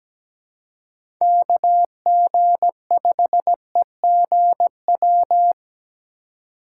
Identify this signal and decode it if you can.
Morse code, heard as a keyed tone, about 20 wpm, 700 Hz. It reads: KG5EGW
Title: Speed Racing - USA 2x3 Call Signs 17wpm